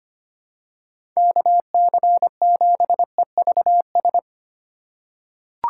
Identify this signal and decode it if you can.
Morse code, heard as a keyed tone, about 25 wpm, 700 Hz. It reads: KC7EVS
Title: USA Call Signs - 8x Character Spacing - 25wpm